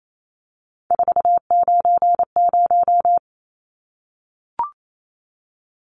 Morse code, heard as a keyed tone, about 28 wpm, 700 Hz. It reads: 490
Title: Numbers - 3-Digit 28wpm